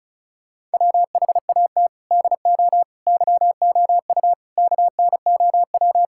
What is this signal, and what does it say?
Morse code, heard as a keyed tone, about 36 wpm, 700 Hz. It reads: WHAT DO YOU KNOW
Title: Sentences from Top 1000 Words - Review Part 6 - 35wpm